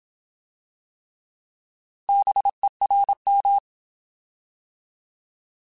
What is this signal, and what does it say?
Morse code, heard as a keyed tone, about 27 wpm, 800 Hz. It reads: BERM
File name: Alphabetical / Alphabetical mp3s / Bs @25